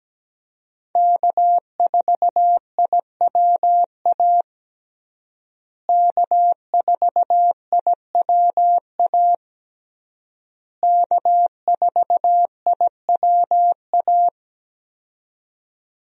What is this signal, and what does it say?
Morse code, heard as a keyed tone, about 17 wpm, 700 Hz. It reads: K4IWA K4IWA K4IWA
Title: USA Call Signs - 5 Characters - Repeated 3x - 17wpm